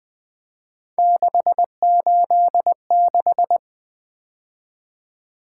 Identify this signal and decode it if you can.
Morse code, heard as a keyed tone, about 20 wpm, 700 Hz. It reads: 686
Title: Three Letter-Number Combinations - 20wpm